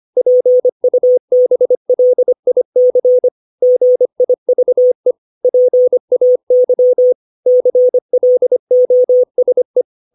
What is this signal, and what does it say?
Morse code, heard as a keyed tone, about 25 wpm, 500 Hz. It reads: PUBLIC GIVE PAY CLOSE
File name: 632 words / 632 Words 25WPM